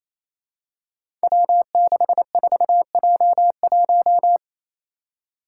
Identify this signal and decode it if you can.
Morse code, heard as a keyed tone, about 29 wpm, 700 Hz. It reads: W64J1
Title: Five Challenging Letter-Number Combinations 28wpm